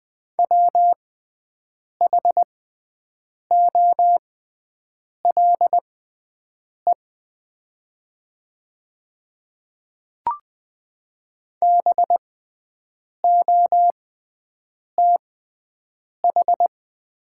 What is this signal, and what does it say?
Morse code, heard as a keyed tone, about 20 wpm, spaced as for 7 wpm, 700 Hz. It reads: WHOLE BOTH
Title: 6x Character Spacing - Top 500 Words - 20wpm